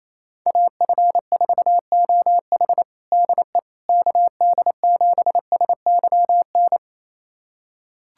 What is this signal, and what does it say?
Morse code, heard as a keyed tone, about 27 wpm, 700 Hz. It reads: AF4OH DE KD7SYN
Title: USA Callsign DE Callsign 28wpm